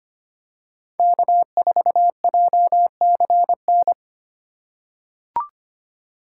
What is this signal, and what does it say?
Morse code, heard as a keyed tone, about 25 wpm, 700 Hz. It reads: K4JCN
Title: USA Call Signs - 5 Characters - 2x Character Spacing - 25wpm